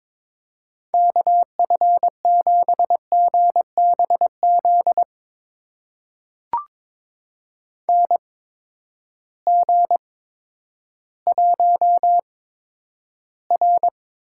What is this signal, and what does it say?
Morse code, heard as a keyed tone, about 22 wpm, spaced as for 11 wpm, 700 Hz. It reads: KF7GBZ NG1R
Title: USA Call Signs - 8x Character Spacing - 22wpm